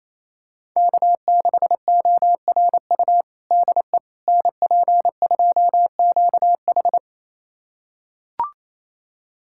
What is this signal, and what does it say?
Morse code, heard as a keyed tone, about 28 wpm, 700 Hz. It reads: K6ORU DE NP2QH
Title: USA Callsign DE Callsign - 5 Characters - 28wpm